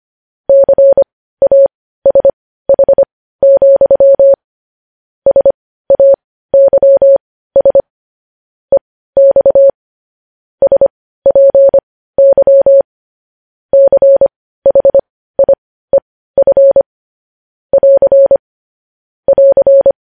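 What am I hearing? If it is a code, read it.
Morse code, heard as a keyed tone, about 25 wpm, spaced as for 15 wpm, 550 Hz. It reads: CASH, SAYS EX SPY CHIEF <AR> <AR>
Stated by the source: Morse code news headlines broadcast.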